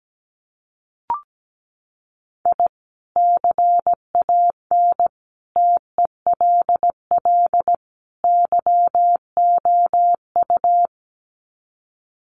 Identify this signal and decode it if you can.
Morse code, heard as a keyed tone, about 17 wpm, 700 Hz. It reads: I CAN TELL YOU
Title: Sets of 4 Words 17wpm